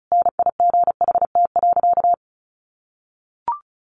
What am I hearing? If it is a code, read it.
Morse code, heard as a keyed tone, about 34 wpm, 700 Hz. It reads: NIGHT.
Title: Sentences from Top 1400 Words 35wpm